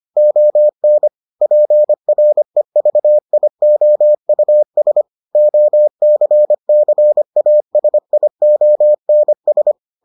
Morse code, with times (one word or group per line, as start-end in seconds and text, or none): 0.2-1.1 ON
1.4-5.0 PREVIOUS
5.3-9.7 OCCASIONS